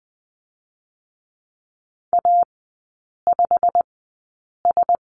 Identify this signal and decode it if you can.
Morse code, heard as a keyed tone, about 20 wpm, spaced as for 8 wpm, 700 Hz. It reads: A5S
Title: 138 Lesson 23e - Letter f - Three Characters 20wpm